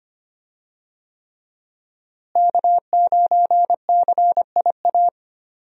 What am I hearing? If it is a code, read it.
Morse code, heard as a keyed tone, about 25 wpm, 700 Hz. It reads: K9CIA
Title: USA Call Signs - 5 Characters - Repeated 3x - 25wpm